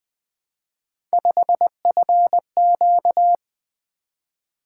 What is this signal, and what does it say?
Morse code, heard as a keyed tone, about 20 wpm, 700 Hz. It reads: 5FQ
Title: Three Letter-Number Combinations - 20wpm